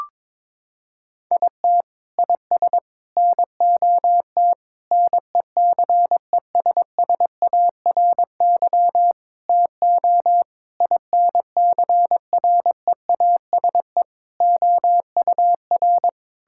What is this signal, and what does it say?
Morse code, heard as a keyed tone, about 22 wpm, 700 Hz. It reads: IT IS NOT NECESSARY TO INCREASE OUR
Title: Sentences from Top 1000 Words - Review Part 6 - 22wpm